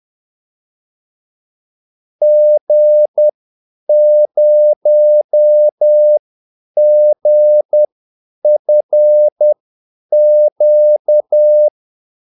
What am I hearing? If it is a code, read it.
Morse code, heard as a keyed tone, about 10 wpm, spaced as for 8 wpm, 600 Hz. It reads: G0GFQ
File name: Callsigns-10-8wpm